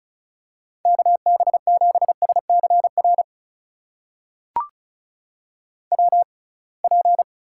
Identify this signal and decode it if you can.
Morse code, heard as a keyed tone, about 36 wpm, spaced as for 27 wpm, 700 Hz. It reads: KB7SCR WP
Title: USA Call Signs - 6 Characters - 6x Character Spacing - 35wpm